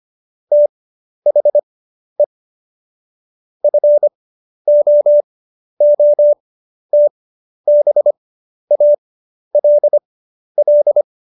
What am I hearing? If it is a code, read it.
Morse code, heard as a keyed tone, about 25 wpm, spaced as for 11 wpm, 590 Hz. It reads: THE FOOTBALL
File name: Jokes-25-char-10-eff